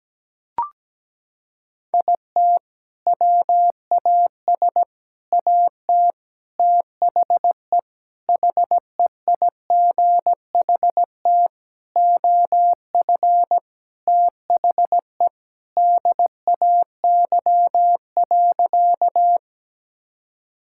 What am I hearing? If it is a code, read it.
Morse code, heard as a keyed tone, about 17 wpm, 700 Hz. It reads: IT WAS AT THE HEIGHT OF THE DAY.
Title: Sentences from Top 1600 Words 17wpm